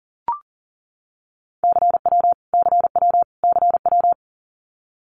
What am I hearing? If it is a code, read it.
Morse code, heard as a keyed tone, about 40 wpm, 700 Hz. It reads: CW CW CW
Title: Abbreviations Repeated 3x - 40wpm